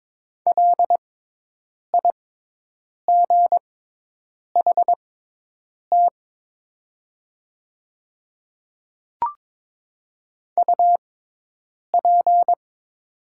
Morse code, heard as a keyed tone, about 22 wpm, spaced as for 8 wpm, 700 Hz. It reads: LIGHT UP
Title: 6x Character Spacing - Top 200 Words - 22wpm